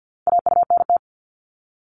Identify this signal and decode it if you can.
Morse code, heard as a keyed tone, about 50 wpm, 700 Hz. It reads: AUNT
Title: Multi-letter Phonemes 50wpm